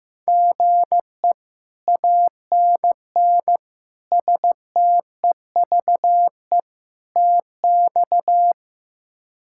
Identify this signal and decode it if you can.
Morse code, heard as a keyed tone, about 15 wpm, 700 Hz. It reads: GE ANN STEVE TX
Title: K1USN SST QSO Practice - 15wpm